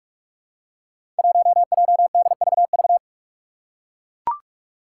Morse code, heard as a keyed tone, about 44 wpm, 700 Hz. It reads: 1JDVV
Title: Five Challenging Letter-Number Combinations 45wpm